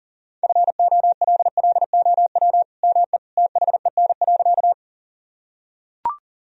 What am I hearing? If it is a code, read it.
Morse code, heard as a keyed tone, about 40 wpm, 700 Hz. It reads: FOLLOW ME THEN.
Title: Sentences from Top 1000 Words - Review Part 1 - 40wpm